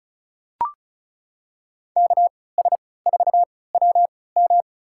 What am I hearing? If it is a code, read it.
Morse code, heard as a keyed tone, about 36 wpm, spaced as for 20 wpm, 700 Hz. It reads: KS4WM
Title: USA Call Signs - 5 Characters - 3x Character Spacing - 35wpm